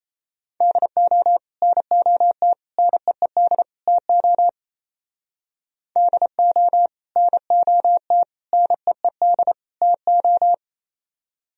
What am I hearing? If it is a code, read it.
Morse code, heard as a keyed tone, about 34 wpm, 700 Hz. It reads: DO NOT NEED TO DO NOT NEED TO
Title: Speed-Racing - Sets of 4 Words 28wpm